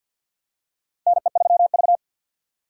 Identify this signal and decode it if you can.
Morse code, heard as a keyed tone, about 52 wpm, 700 Hz. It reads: NE3V
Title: USA Call Signs - 3x Character Spacing - 50wpm